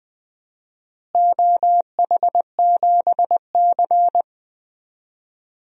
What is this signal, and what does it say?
Morse code, heard as a keyed tone, about 21 wpm, 700 Hz. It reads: OH7C
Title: DXCC-100 Call Signs - 4 Characters - Repeated 3x - 20wpm